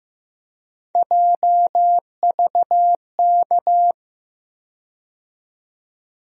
Morse code, heard as a keyed tone, about 15 wpm, 700 Hz. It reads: JVK
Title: USA Call Sign Postfixes - 3 Letters 15wpm